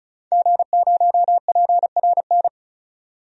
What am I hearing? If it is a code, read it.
Morse code, heard as a keyed tone, about 36 wpm, 700 Hz. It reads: G0PRN
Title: DXCC-100 Call Signs - 5 Characters - 35wpm